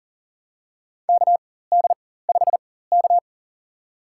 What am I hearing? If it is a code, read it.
Morse code, heard as a keyed tone, about 40 wpm, spaced as for 19 wpm, 700 Hz. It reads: KD5K
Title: USA Call Signs - 4 Characters - 4x Character Spacing - 40wpm